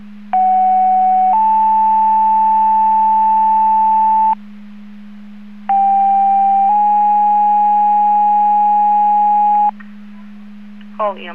Two-tone page, heard as a one-second tone, then a three-second tone, about 750 then 880 Hz, about 790 then 830 Hz. It is heard from 0.3 to 9.7 s.